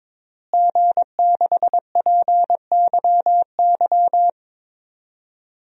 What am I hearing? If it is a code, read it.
Morse code, heard as a keyed tone, about 22 wpm, 700 Hz. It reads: G6PYY
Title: DXCC-100 Call Signs - 5 Characters - Repeated 3x - 22wpm